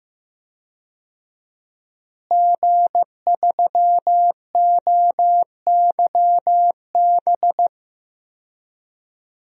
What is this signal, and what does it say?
Morse code, heard as a keyed tone, about 15 wpm, 700 Hz. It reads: G3OYB
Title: DXCC-100 Call Signs - 5 Characters - Repeated 2x - 15wpm